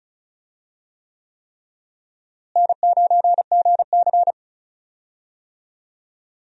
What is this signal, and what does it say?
Morse code, heard as a keyed tone, about 36 wpm, 700 Hz. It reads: N9GC
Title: USA Call Signs - 4 Characters - Repeated 3x - 35wpm